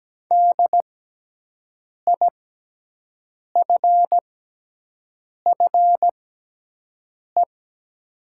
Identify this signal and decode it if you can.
Morse code, heard as a keyed tone, about 17 wpm, spaced as for 6 wpm, 700 Hz. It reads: DIFFE
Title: 6x Character Spacing - Top 300 Words - 17wpm